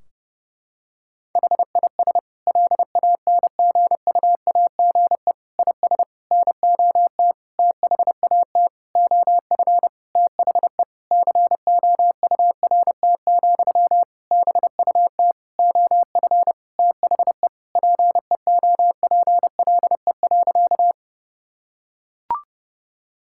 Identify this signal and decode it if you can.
Morse code, heard as a keyed tone, about 30 wpm, 700 Hz. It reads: HIS LANGUAGE IS NOT THAT OF THE COURT, BUT OF THE PEOPLE.
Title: Sentences from Top 1600 Words 30wpm